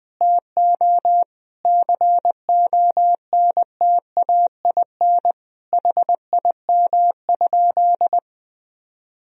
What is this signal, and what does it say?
Morse code, heard as a keyed tone, about 21 wpm, 700 Hz. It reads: TO CONTAIN HIM?
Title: Sentences from Top 1000 Words - Review Part 2 - 20wpm